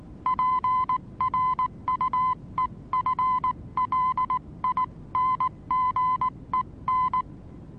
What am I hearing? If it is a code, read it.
Morse code, heard as a keyed tone, about 19 wpm, 1020 Hz. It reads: PRUEFLINGEN
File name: repos / Morsekurs / DJ4UF Morsen26